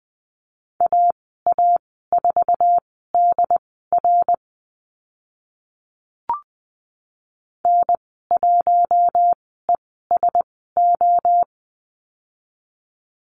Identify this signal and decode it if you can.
Morse code, heard as a keyed tone, about 20 wpm, spaced as for 14 wpm, 700 Hz. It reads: AA4DR N1ESO
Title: 076 Lesson 13g - Letter L - Call Signs 20wpm